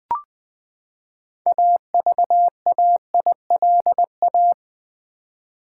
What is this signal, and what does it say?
Morse code, heard as a keyed tone, about 20 wpm, 700 Hz. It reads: AVAILA
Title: Build A Word - 6 Letters At A Time - 12-Letter Words - 20wpm